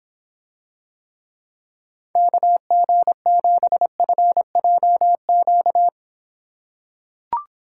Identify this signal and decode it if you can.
Morse code, heard as a keyed tone, about 24 wpm, 700 Hz. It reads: KG7FJQ